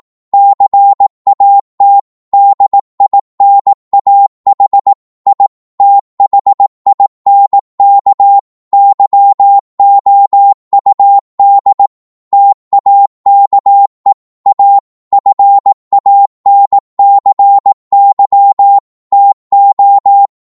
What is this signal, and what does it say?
Morse code, heard as a keyed tone, about 18 wpm, 800 Hz. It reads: CAT DINAH I THINK YOUD TAKE A FANCY TO
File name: Morse / psg-18wpm